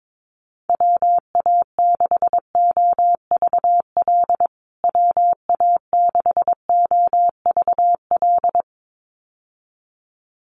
Morse code, heard as a keyed tone, about 22 wpm, 700 Hz. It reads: WA6OVL WA6OVL
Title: USA Call Signs Repeated 2x 22wpm